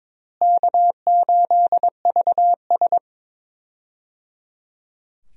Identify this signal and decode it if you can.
Morse code, heard as a keyed tone, about 22 wpm, 700 Hz. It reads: K8VS
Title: USA Call Signs - 4 Characters - Repeated 3x - 22wpm